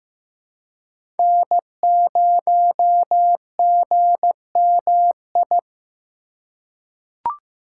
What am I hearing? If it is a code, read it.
Morse code, heard as a keyed tone, about 16 wpm, 700 Hz. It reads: N0GMI
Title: USA Call Signs - 6x Character Spacing - 15wpm